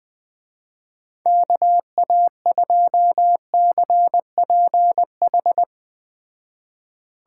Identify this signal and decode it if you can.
Morse code, heard as a keyed tone, about 20 wpm, 700 Hz. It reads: KA2CPH
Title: Speed Racing - USA 2x3 Call Signs 17wpm